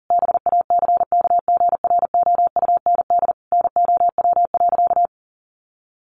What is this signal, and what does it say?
Morse code, heard as a keyed tone, about 40 wpm, 700 Hz. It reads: BACKGROUND NOW.
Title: Sentences from Top 1300 Words 40wpm